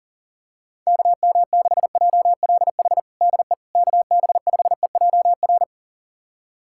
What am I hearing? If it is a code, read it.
Morse code, heard as a keyed tone, about 37 wpm, 700 Hz. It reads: KM6JLH DE KB5EJR
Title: USA Callsign DE Callsign - 6 Characters - 40wpm